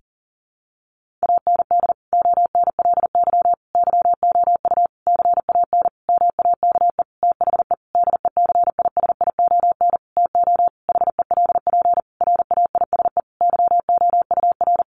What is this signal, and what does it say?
Morse code, heard as a keyed tone, about 40 wpm, 700 Hz. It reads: AND ONLY YOU CAN MAKE THE DECISION TO HELP RAISE YOUR
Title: Sentences from Top 1400 Words 40wpm